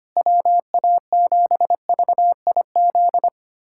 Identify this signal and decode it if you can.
Morse code, heard as a keyed tone, about 25 wpm, 700 Hz. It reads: WA7VIZ
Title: USA Callsign DE Callsign - 6 Characters - 25wpm